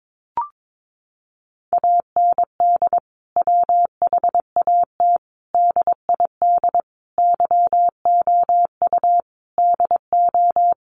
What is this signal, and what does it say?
Morse code, heard as a keyed tone, about 22 wpm, 700 Hz. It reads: AND WHAT DID YOU DO
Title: Sentences from Top 100 Words 22wpm